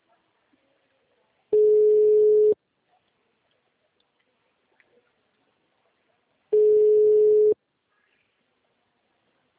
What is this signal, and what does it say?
Ringback tone (Europe, 425 Hz).